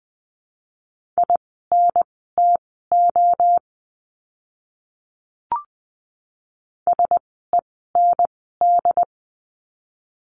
Morse code, heard as a keyed tone, about 20 wpm, spaced as for 14 wpm, 700 Hz. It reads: INTO SEND
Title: 075 Lesson 13f - Letter L - Words 20wpm